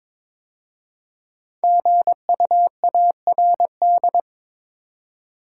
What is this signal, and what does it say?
Morse code, heard as a keyed tone, about 22 wpm, 700 Hz. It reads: GUARD
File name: 5 Letter Words - First 3 Letters Given - 22wpm